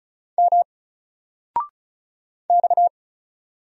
Morse code, heard as a keyed tone, about 33 wpm, 700 Hz. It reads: M X
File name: Single-Letter-Number-35wpm